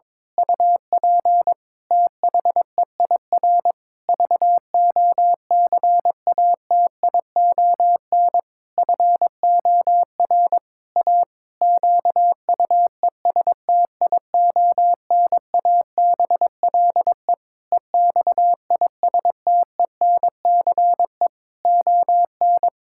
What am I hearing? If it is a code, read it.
Morse code, heard as a keyed tone, about 22 wpm, 700 Hz. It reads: UP THEIR VOCATION FOR A QUESTIONABLE EXISTENCE ON
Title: Pangram Challenge v2 - 22wpm